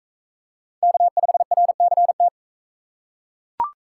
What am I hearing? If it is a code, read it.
Morse code, heard as a keyed tone, about 40 wpm, 700 Hz. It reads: K5RCT